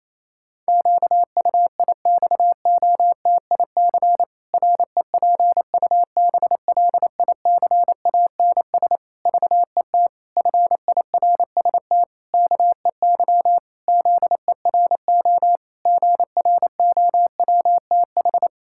Morse code, heard as a keyed tone, about 28 wpm, 700 Hz. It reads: QUIXOTIC REPUBLICANS VET FIRST KEY ZERO GROWTH
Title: Pangram Challenge v2 - 28wpm